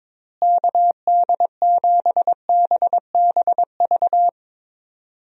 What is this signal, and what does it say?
Morse code, heard as a keyed tone, about 22 wpm, 700 Hz. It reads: KD7BBV
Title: USA Call Signs - 6 Characters - Repeated 2x - 22wpm